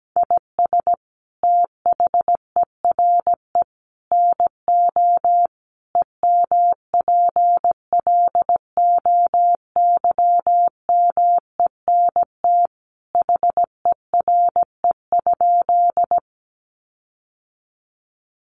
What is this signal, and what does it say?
Morse code, heard as a keyed tone, about 17 wpm, 700 Hz. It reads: IS THERE NO EMPLOYMENT HERE?
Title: Sentences from Top 1500 Words 17wpm